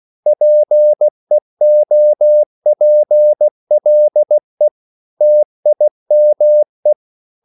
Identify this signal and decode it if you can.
Morse code, heard as a keyed tone, about 16 wpm, 600 Hz. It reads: PEOPLE TIME
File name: CW / Practice / 500MostCommonEnglishWords16